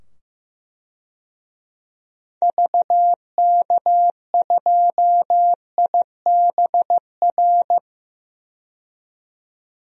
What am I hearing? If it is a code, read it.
Morse code, heard as a keyed tone, about 15 wpm, 700 Hz. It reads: VK2IBR
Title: DXCC-100 Call Signs - 6 Characters - 15wpm